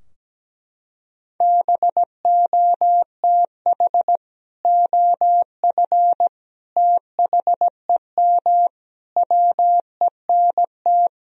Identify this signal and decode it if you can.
Morse code, heard as a keyed tone, about 17 wpm, 700 Hz. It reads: BOTH OF THEM WENT
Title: Sentences from Top 1000 Words - Review Part 2 - 17wpm